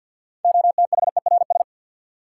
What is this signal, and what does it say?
Morse code, heard as a keyed tone, about 53 wpm, 700 Hz. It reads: OTHERS
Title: Sentences from Top 1000 Words - Review Part 2 - New Word Drill - 50wpm